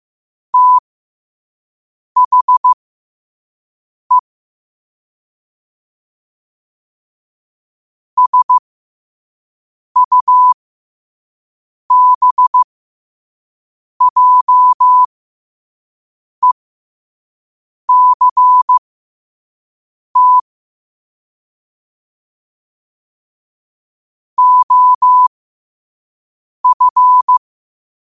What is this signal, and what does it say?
Morse code, heard as a keyed tone, about 15 wpm, spaced as for 5 wpm, 1000 Hz. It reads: THE SUBJECT OF